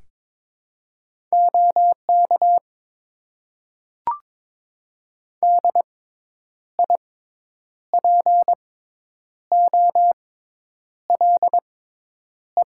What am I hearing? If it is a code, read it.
Morse code, heard as a keyed tone, about 22 wpm, spaced as for 9 wpm, 700 Hz. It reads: OK DIPOLE
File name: Top 90 QSO Elements - 6x Character Spacing - 22wpm